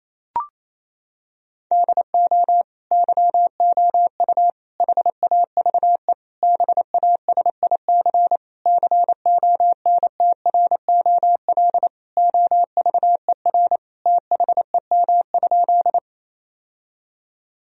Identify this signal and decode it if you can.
Morse code, heard as a keyed tone, about 28 wpm, 700 Hz. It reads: DO YOU HAVE BASIC CONTROL OVER THEM?
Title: Sentences from Top 1000 Words - Review Part 3 - 28wpm